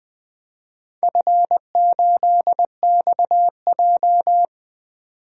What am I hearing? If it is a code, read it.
Morse code, heard as a keyed tone, about 20 wpm, 700 Hz. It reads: F8XJ
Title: DXCC-100 Call Signs - 4 Characters - 20wpm